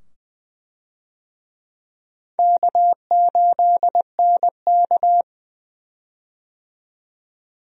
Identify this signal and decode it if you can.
Morse code, heard as a keyed tone, about 20 wpm, 700 Hz. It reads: K8NK
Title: USA Call Signs - 4 Characters - 20wpm